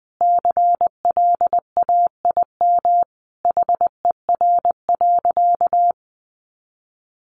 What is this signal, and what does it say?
Morse code, heard as a keyed tone, about 20 wpm, 700 Hz. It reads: CLAIM HER.
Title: Sentences from Top 1500 Words 20wpm